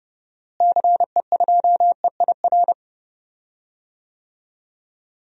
Morse code, heard as a keyed tone, about 31 wpm, 700 Hz. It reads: CE2EIR
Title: DXCC-100 Call Signs - 6 Characters - Repeated 3x - 30wpm